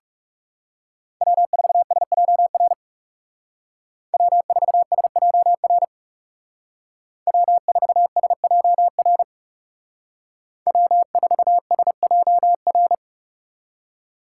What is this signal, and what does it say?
Morse code, heard as a keyed tone, about 37 wpm, 700 Hz. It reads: W4SJR W4SJR W4SJR W4SJR
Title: Speed Racing - USA Call Signs 30wpm